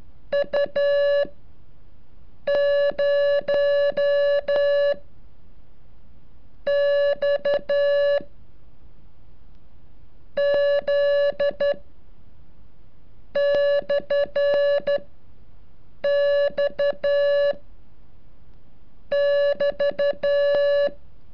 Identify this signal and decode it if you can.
Morse code, heard as a keyed tone, about 12 wpm, spaced as for 6 wpm, 570 Hz. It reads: U 0 X Z /X <BT>
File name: curso / ejercicio 35